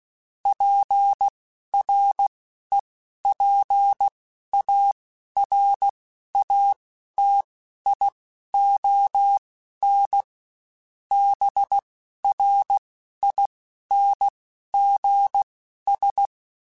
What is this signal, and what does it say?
Morse code, heard as a keyed tone, about 16 wpm, spaced as for 11 wpm, 800 Hz. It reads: PREPARATION BRINGS
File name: Quotes Spaced / Quotes Spaced-mp3s / Persperation sp @15